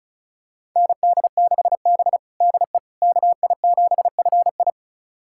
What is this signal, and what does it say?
Morse code, heard as a keyed tone, about 36 wpm, 700 Hz. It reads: ND6B DE KI7FI
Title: USA Callsign DE Callsign 35wpm